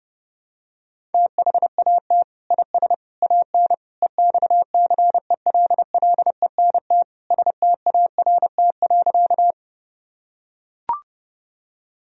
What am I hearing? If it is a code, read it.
Morse code, heard as a keyed tone, about 30 wpm, 700 Hz. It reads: THAT IS AN EXCELLENT START.
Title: Sentences from Top 1000 Words - Review Part 1 - 30wpm